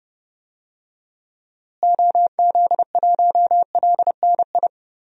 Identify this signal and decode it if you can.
Morse code, heard as a keyed tone, about 32 wpm, 700 Hz. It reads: OZ1LNI
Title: DXCC-100 Call Signs - 6 Characters - Repeated 3x - 30wpm